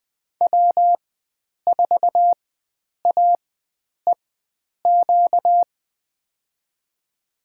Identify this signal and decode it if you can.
Morse code, heard as a keyed tone, about 20 wpm, spaced as for 9 wpm, 700 Hz. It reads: W4AEQ
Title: USA Call Signs - 5 Characters - 4x Character Spacing - 20wpm